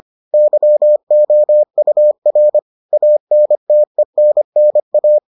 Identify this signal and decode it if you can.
Morse code, heard as a keyed tone, about 25 wpm, 600 Hz. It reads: YOUR ANTENNA